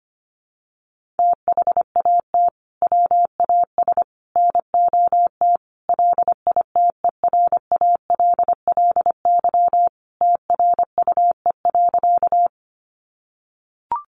Morse code, heard as a keyed tone, about 25 wpm, 700 Hz. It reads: THAT WAS NOT LITERALLY TRUE.
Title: Sentences from Top 1400 Words 25wpm